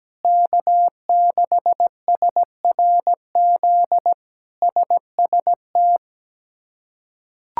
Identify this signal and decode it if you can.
Morse code, heard as a keyed tone, about 17 wpm, 700 Hz. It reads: K6SRZ SST
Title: K1USN SST QSO Practice - 17wpm